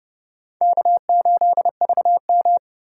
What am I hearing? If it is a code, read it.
Morse code, heard as a keyed tone, about 30 wpm, 700 Hz. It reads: K8VM
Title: USA Call Signs - 4 Characters - 2x Character Spacing - 30wpm